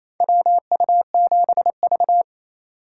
Morse code, heard as a keyed tone, about 28 wpm, 700 Hz. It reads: WU7V
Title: USA Call Signs - 8x Character Spacing - 28wpm